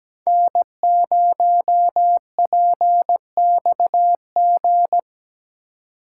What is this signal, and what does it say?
Morse code, heard as a keyed tone, about 20 wpm, 700 Hz. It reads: N0PXG
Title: Speed Racing - USA 1x3 Call Signs 17wpm